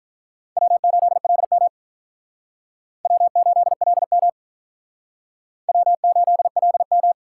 Morse code, heard as a keyed tone, about 45 wpm, 700 Hz. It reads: W8LM W8LM W8LM